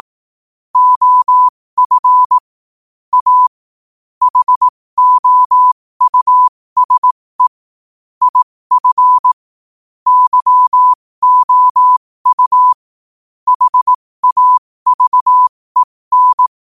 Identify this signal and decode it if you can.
Morse code, heard as a keyed tone, about 18 wpm, spaced as for 15 wpm, 1000 Hz. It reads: OF A HOUSE IF YOU HAVEN
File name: qotdfiles / 15wpm